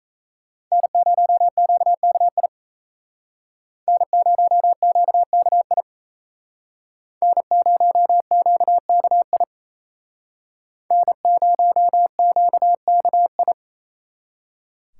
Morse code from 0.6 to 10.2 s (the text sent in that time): N0QKI N0QKI N0QKI